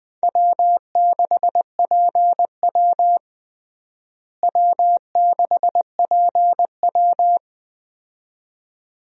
Morse code, heard as a keyed tone, about 20 wpm, 700 Hz. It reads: W6PW W6PW
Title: USA Call Signs - 4 Characters - Repeated 2x - 20wpm